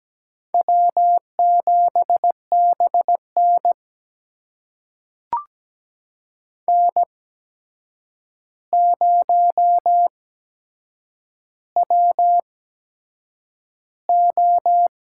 Morse code, heard as a keyed tone, about 17 wpm, spaced as for 7 wpm, 700 Hz. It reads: W7BN N0WO
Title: USA Call Signs - 4 Characters - 8x Character Spacing - 17wpm